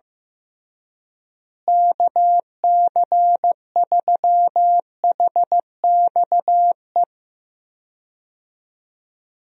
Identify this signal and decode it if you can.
Morse code, heard as a keyed tone, about 15 wpm, 700 Hz. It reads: KC3HXE
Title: USA Call Signs - 6 Characters - Repeated 3x - 15wpm